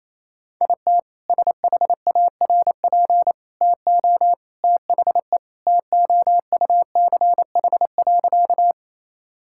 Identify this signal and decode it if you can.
Morse code, heard as a keyed tone, about 28 wpm, 700 Hz. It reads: IT SHARP TO THE TOUCH.
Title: Sentences from Top 1600 Words 28wpm